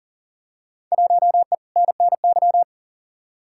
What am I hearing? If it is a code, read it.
Morse code, heard as a keyed tone, about 40 wpm, 700 Hz. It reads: 1E NNY
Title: Field Day Exchanges - 40wpm